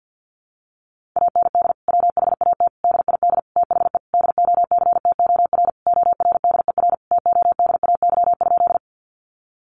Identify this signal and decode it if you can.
Morse code, heard as a keyed tone, about 50 wpm, 700 Hz. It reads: AND WHAT DID THE DOCTOR ORDER TODAY?
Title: Sentences from Top 1500 Words 50wpm